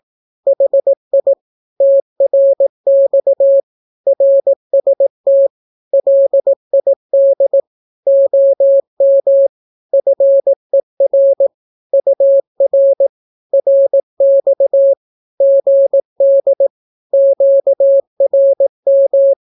Morse code, heard as a keyed tone, about 18 wpm, 550 Hz. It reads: HI TRX RST LID OM FER UR RX GD QRM
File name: naukapodstaw / skroty / skroty